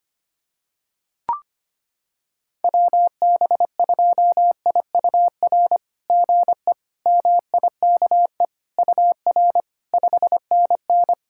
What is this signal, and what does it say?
Morse code, heard as a keyed tone, about 25 wpm, 690 Hz. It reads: WB2IUR GE MIKE UR 5NN
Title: SOTA QSO Practice 25wpm